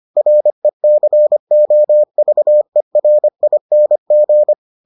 Morse code, heard as a keyed tone, about 25 wpm, 600 Hz. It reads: RECOVERING